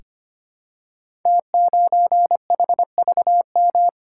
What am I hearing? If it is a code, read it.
Morse code, heard as a keyed tone, about 25 wpm, 700 Hz. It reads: T9HVM
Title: Code Groups - 5 Letter:Numbers - 25wpm